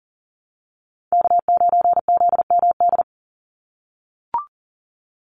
Morse code, heard as a keyed tone, about 40 wpm, 700 Hz. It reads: K9ZMD
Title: NA SOTA Chaser Callsigns 40wpm